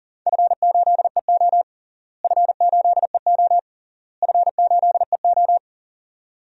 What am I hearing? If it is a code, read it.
Morse code, heard as a keyed tone, about 40 wpm, 700 Hz. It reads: F8EO F8EO F8EO
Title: DXCC-100 Call Signs - 4 Characters - Repeated 3x - 40wpm